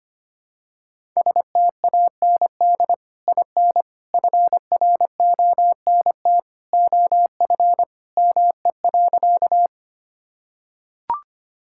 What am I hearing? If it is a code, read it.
Morse code, heard as a keyed tone, about 25 wpm, 700 Hz. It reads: STAND IN FRONT OF ME.
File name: practice / 25wpm / Sentences with 2-Word Phrases - 25wpm